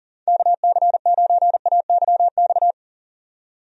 Morse code, heard as a keyed tone, about 40 wpm, 700 Hz. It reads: KC9AYX
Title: USA Call Signs - 6 Characters - 40wpm